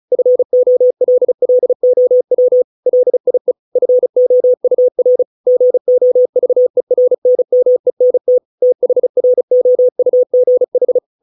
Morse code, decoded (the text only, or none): FOLLOW LIE FOUR GOVERNMENT THROUGH